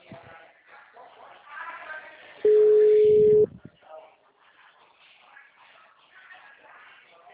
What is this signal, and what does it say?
Ringback tone (Europe, 425 Hz), with speech.
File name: out- -2003- - - .54637